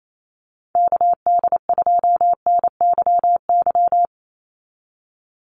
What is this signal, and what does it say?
Morse code, heard as a keyed tone, about 28 wpm, 700 Hz. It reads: KD2NYY
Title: USA Call Signs 28wpm